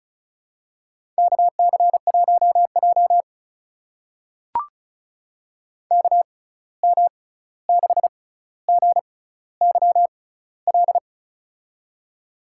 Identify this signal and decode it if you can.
Morse code, heard as a keyed tone, about 36 wpm, spaced as for 16 wpm, 700 Hz. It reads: KC1J KM6GYL
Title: USA Call Signs - 6x Character Spacing - 35wpm